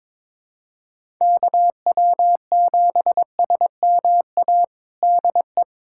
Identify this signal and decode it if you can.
Morse code, heard as a keyed tone, about 22 wpm, 700 Hz. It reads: KW7SMA DE
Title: USA Callsign DE Callsign 22wpm